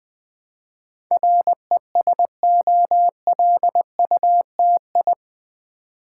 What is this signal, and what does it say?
Morse code, heard as a keyed tone, about 20 wpm, 700 Hz. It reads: RESOLUTI
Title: Build A Word - 4 Letters At A Time - 10-Letter Words - 20wpm